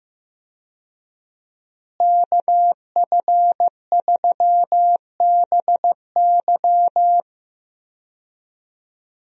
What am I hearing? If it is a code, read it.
Morse code, heard as a keyed tone, about 15 wpm, 700 Hz. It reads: KF3BY
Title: USA Call Signs - 5 Characters - Repeated 3x - 15wpm